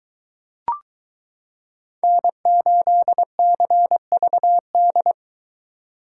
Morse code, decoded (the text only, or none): N8CVD